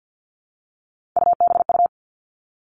Morse code, heard as a keyed tone, about 50 wpm, 700 Hz. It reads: UBU
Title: Three Challenging Letter-Number Combinations 50wpm